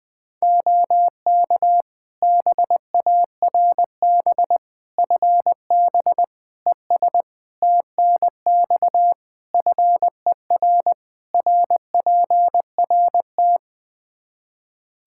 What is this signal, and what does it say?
Morse code, heard as a keyed tone, about 21 wpm, 700 Hz. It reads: OK BARB FB ES TNX FER RPRT
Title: Practice QSO - Exchange 2 - Info - 20wpm